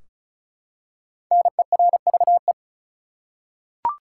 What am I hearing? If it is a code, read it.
Morse code, heard as a keyed tone, about 35 wpm, 700 Hz. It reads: NERVE
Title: Five Letter Words - 35wpm